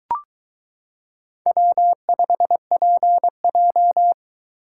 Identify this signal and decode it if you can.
Morse code, heard as a keyed tone, about 23 wpm, 700 Hz. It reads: W5PJ